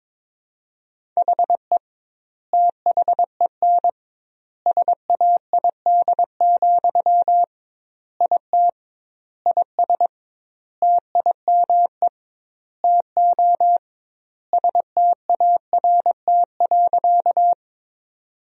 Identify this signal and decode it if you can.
Morse code, heard as a keyed tone, about 22 wpm, 700 Hz. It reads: HE THEN SAID, IT IS TIME TO START.
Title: 2x Word-Spacing - Sentences from Top 300 Words - 22wpm